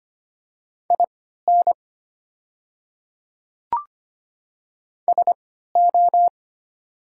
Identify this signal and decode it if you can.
Morse code, heard as a keyed tone, about 25 wpm, spaced as for 14 wpm, 700 Hz. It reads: IN SO
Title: 3x Character Spacing - Top 100 Words - 25wpm